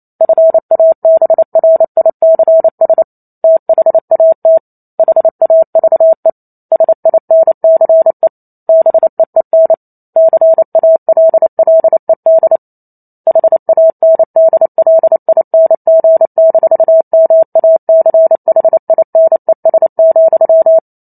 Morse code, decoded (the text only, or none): FABRICS THAT HAVE SINCE BEEN CALLED HANDLING-MACHINES,